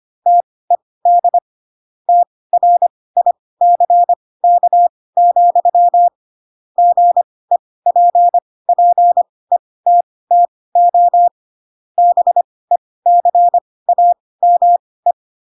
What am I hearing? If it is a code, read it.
Morse code, heard as a keyed tone, about 25 wpm, spaced as for 18 wpm, 700 Hz. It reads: TED TRICK, GEPPETTO BECAME
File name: Pinocchio ch3 12 25 ik1hge